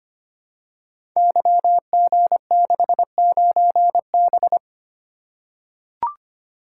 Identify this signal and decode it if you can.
Morse code, heard as a keyed tone, about 25 wpm, 700 Hz. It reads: YG69B
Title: Code Groups - 5 Letter:Numbers - 25wpm